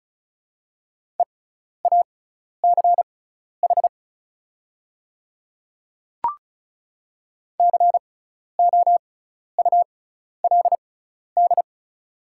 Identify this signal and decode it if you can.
Morse code, heard as a keyed tone, about 36 wpm, spaced as for 12 wpm, 690 Hz. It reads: EACH COULD
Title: 6x Character Spacing - Top 100 Words - 35wpm